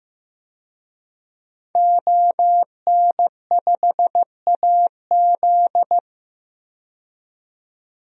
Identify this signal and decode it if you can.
Morse code, heard as a keyed tone, about 15 wpm, 700 Hz. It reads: ON5AZ
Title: DXCC-100 Call Signs - 5 Characters - 15wpm